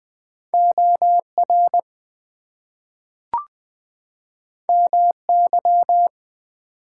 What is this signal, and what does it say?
Morse code, heard as a keyed tone, about 20 wpm, 700 Hz. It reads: OR MY
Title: Two Letter Words - 20wpm